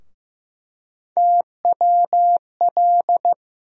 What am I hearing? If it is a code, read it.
Morse code, heard as a keyed tone, about 15 wpm, 700 Hz. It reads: TWL
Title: USA Call Sign Postfixes - 3 Letters 15wpm